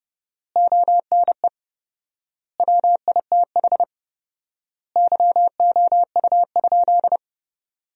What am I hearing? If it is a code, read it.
Morse code, heard as a keyed tone, about 32 wpm, 700 Hz. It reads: ONE WITH YOU?
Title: 4x Word-Spacing - Sentences from Top 500 Words - 30wpm